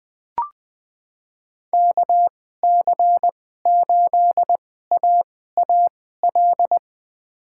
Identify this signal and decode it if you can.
Morse code, heard as a keyed tone, about 20 wpm, spaced as for 14 wpm, 700 Hz. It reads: KC8AAL
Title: USA Call Signs - 6 Characters - 2x Character Spacing - 20wpm